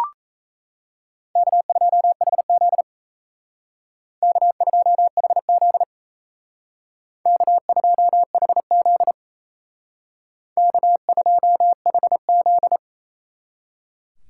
Morse code, heard as a keyed tone, about 32 wpm, 700 Hz. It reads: K2HZ K2HZ K2HZ K2HZ
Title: Speed Racing - USA 1x2 Call Signs 28wpm